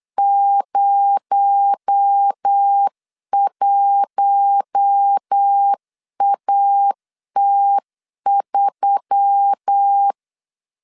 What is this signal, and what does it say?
Morse code, heard as a keyed tone, about 9 wpm, 800 Hz. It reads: 01AT3
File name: CW / DET / 1 6 1500C S C 8wpm 5219